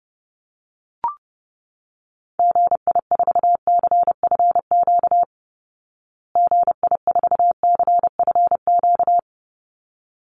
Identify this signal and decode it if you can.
Morse code, heard as a keyed tone, about 30 wpm, 700 Hz. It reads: GI4CFQ GI4CFQ
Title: Call Signs Repeated 3x - 30wpm